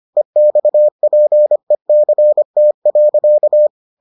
Morse code, heard as a keyed tone, about 25 wpm, 600 Hz. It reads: EXPECT.